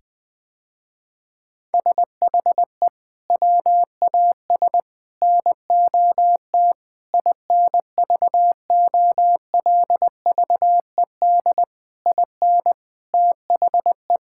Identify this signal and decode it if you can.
Morse code, heard as a keyed tone, about 20 wpm, 700 Hz. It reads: SHE WAS NOT INVOLVED IN THE
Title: Sentences with 6-Word Phrases - Spoken Beginning - 20wpm